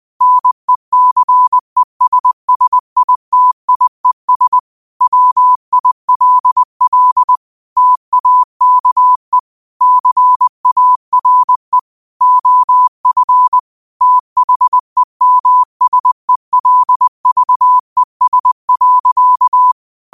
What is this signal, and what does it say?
Morse code, heard as a keyed tone, about 20 wpm, 1000 Hz. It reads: NECESSITIES WILL TAKE CARE OF THEMSELVES.